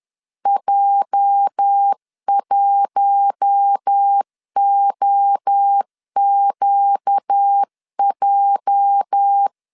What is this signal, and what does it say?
Morse code, heard as a keyed tone, about 10 wpm, 800 Hz. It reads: J1OQJ
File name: LEZIONI / 8 YJ45 / 8 YJ 10wpm 2651